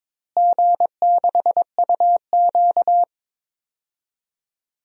Morse code, heard as a keyed tone, about 22 wpm, 700 Hz. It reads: G6UQ
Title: DXCC-100 Call Signs - 4 Characters - Repeated 2x - 22wpm